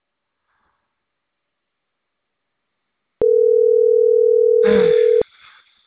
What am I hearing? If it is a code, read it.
Ringback tone (North America), 440 + 480 Hz.